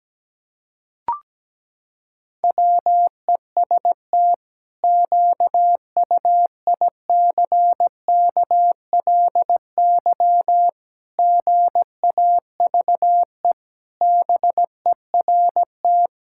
Morse code, heard as a keyed tone, about 17 wpm, 700 Hz. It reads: WEST QUICKLY GAVE BERT
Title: Pangram Challenge v2 - 17wpm